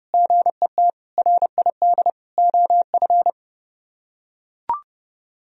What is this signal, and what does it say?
Morse code, heard as a keyed tone, about 29 wpm, 700 Hz. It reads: GET RID OF